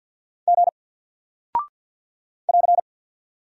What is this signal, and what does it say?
Morse code, heard as a keyed tone, about 48 wpm, 700 Hz. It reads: G <AR>
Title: Single Character (Letter:Number:Punctuation:Prosign) Alt 50wpm